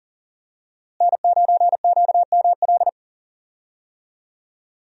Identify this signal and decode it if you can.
Morse code, heard as a keyed tone, about 40 wpm, 700 Hz. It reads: N9QML
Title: USA Call Signs - 5 Characters - Repeated 2x - 40wpm